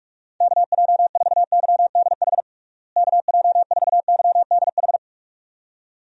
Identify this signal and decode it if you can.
Morse code, heard as a keyed tone, about 47 wpm, 700 Hz. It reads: KJ4YDH KJ4YDH
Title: USA Call Signs - 6 Characters - Repeated 2x - 45wpm